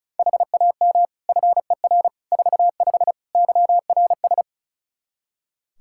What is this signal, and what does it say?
Morse code, heard as a keyed tone, about 34 wpm, 700 Hz. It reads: HAM FER 45 YRS
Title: Practice QSO - Exchange 3 - More Info - 35wpm